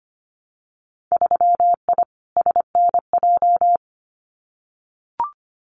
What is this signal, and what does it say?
Morse code, heard as a keyed tone, about 25 wpm, 700 Hz. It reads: 3I SNJ
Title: Winter Field Day Exchanges Encore 25wpm